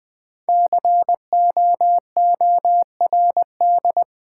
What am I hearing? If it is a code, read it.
Morse code, heard as a keyed tone, about 20 wpm, 700 Hz. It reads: COORD
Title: Build A Word - 5 Letters At A Time - 12-Letter Words - 20wpm